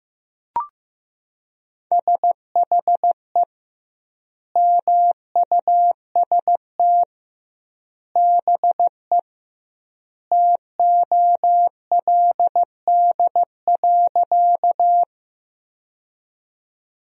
Morse code, heard as a keyed tone, about 15 wpm, 700 Hz. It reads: SHE MUST BE TOLD.
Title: 2x Word-Spacing - Sentences from Top 400 Words - 15wpm